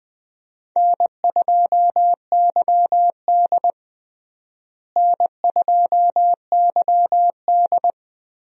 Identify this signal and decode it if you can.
Morse code, heard as a keyed tone, about 20 wpm, 700 Hz. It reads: N2YD N2YD
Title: USA Call Signs - 4 Characters - Repeated 3x - 20wpm